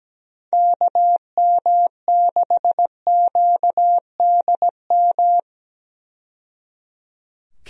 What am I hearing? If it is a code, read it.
Morse code, heard as a keyed tone, about 17 wpm, 700 Hz. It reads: KM6QDM
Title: USA Call Signs - 6 Characters - 17wpm